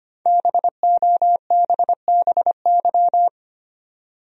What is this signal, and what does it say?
Morse code, heard as a keyed tone, about 25 wpm, 700 Hz. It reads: BOBBY
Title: USA Common Names - 25wpm